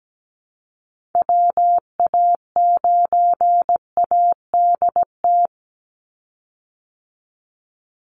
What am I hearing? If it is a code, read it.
Morse code, heard as a keyed tone, about 17 wpm, 700 Hz. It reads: WA9ADT
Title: Call Signs Repeated 3x - 17wpm